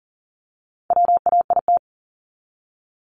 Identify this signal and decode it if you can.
Morse code, heard as a keyed tone, about 40 wpm, 700 Hz. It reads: WAIT
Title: ING Suffix - Top 100 Words with Base Words 40wpm